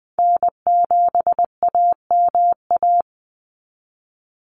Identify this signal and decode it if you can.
Morse code, heard as a keyed tone, about 20 wpm, 700 Hz. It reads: N7AMA
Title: NA SOTA Chaser Callsigns 20wpm